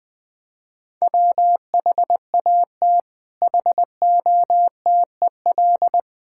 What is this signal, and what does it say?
Morse code, heard as a keyed tone, about 21 wpm, 700 Hz. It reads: WHAT HOTEL
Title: Sentences from Top 1600 Words 20wpm